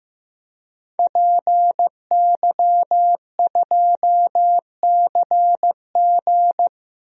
Morse code, heard as a keyed tone, about 15 wpm, 700 Hz. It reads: PY2CG
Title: DXCC-100 Call Signs - 5 Characters - 15wpm